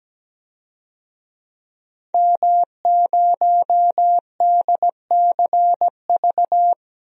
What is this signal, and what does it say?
Morse code, heard as a keyed tone, about 17 wpm, 700 Hz. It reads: M0DCV
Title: DXCC-100 Call Signs - 5 Characters - 17wpm